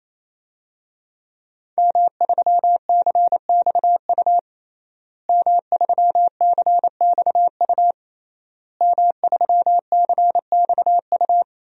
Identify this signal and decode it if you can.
Morse code, heard as a keyed tone, about 27 wpm, 700 Hz. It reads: M3CXU M3CXU M3CXU
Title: DXCC-100 Call Signs - 5 Characters - Repeated 3x - 28wpm